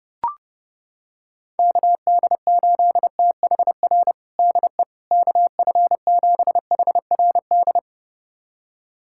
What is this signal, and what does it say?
Morse code, heard as a keyed tone, about 30 wpm, 700 Hz. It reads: KD8THR DE KF7HRD
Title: USA Callsign DE Callsign - 6 Characters - 30wpm